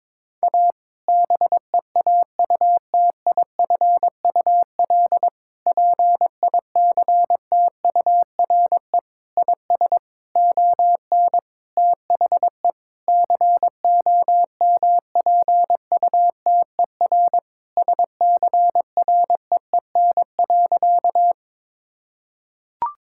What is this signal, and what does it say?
Morse code, heard as a keyed tone, about 22 wpm, 700 Hz. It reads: A BEAUTIFUL PICTURE IS ON THE COMPUTER SCREEN.
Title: Sentences from Top 1000 Words - Review Part 6 - 22wpm